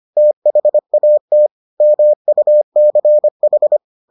Morse code, heard as a keyed tone, about 25 wpm, 600 Hz. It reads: THAT MUCH